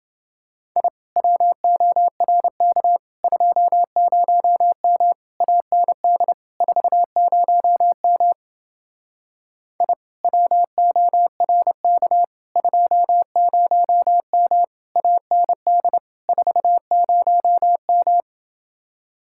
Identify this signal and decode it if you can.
Morse code, heard as a keyed tone, about 29 wpm, 700 Hz. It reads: I WORK 20M AND 40M I WORK 20M AND 40M